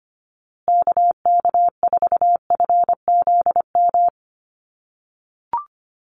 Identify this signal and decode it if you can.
Morse code, heard as a keyed tone, about 25 wpm, 700 Hz. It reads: KK4FZM
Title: USA Call Signs Repeated 2x 25wpm